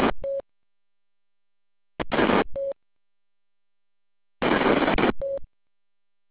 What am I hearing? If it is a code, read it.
There is no code.